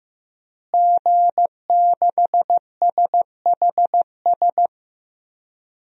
Morse code, heard as a keyed tone, about 15 wpm, 700 Hz. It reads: G6SHS
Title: DXCC-100 Call Signs - 5 Characters - Repeated 3x - 15wpm